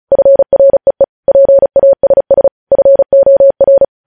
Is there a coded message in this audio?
Yes: Morse code from 0.1 to 3.8 s, heard as a keyed tone, about 36 wpm, 550 Hz. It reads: FREE PASS FOR